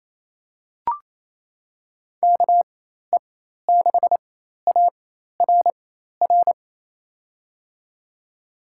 Morse code, heard as a keyed tone, about 28 wpm, spaced as for 13 wpm, 700 Hz. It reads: KE6ARR
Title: USA Call Signs - 6 Characters - 4x Character Spacing - 28wpm